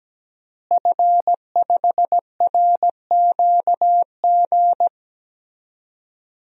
Morse code, heard as a keyed tone, about 17 wpm, 700 Hz. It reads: F5RQG
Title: DXCC-100 Call Signs - 5 Characters - Repeated 3x - 17wpm